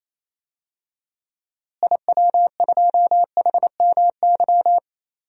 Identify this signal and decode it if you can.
Morse code, heard as a keyed tone, about 27 wpm, 700 Hz. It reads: IW2HMY
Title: DXCC-100 Call Signs - 6 Characters - Repeated 2x - 28wpm